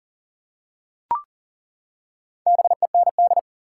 Morse code, heard as a keyed tone, about 40 wpm, 700 Hz. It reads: BEND
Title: Four Letter Words - 40wpm